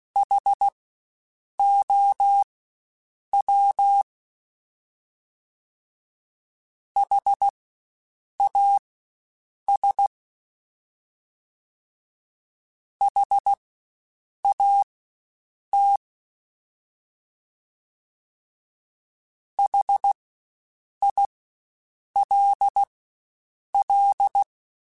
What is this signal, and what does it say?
Morse code, heard as a keyed tone, about 16 wpm, spaced as for 7 wpm, 800 Hz. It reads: HOW HAS HAT HILL
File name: Spaced Alphas / Spaced Alphas mp3s / Hs sp @15